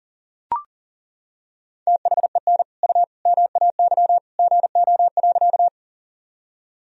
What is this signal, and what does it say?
Morse code, heard as a keyed tone, about 40 wpm, 700 Hz. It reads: THEN U MAY GO.
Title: Sentences from Top 100 Words and Abbreviations 40wpm